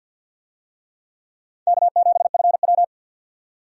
Morse code, heard as a keyed tone, about 52 wpm, 700 Hz. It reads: K7FW
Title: USA Call Signs - 4 Characters - 50wpm